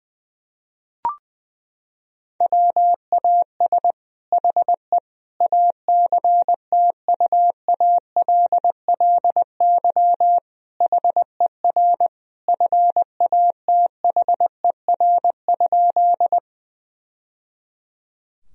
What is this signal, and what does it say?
Morse code, heard as a keyed tone, about 20 wpm, 700 Hz. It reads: WAS HE ACTUALLY HER FATHER?
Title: Sentences from Top 1000 Words - Review Part 4 - 20wpm